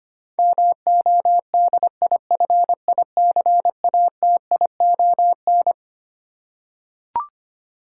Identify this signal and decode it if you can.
Morse code, heard as a keyed tone, about 25 wpm, 700 Hz. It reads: MODIFICATION
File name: Twelve Letter Words - 25wpm